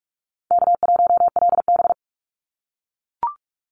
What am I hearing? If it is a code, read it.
Morse code, heard as a keyed tone, about 44 wpm, 690 Hz. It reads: K1LB
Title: NA SOTA Chaser Callsigns 45wpm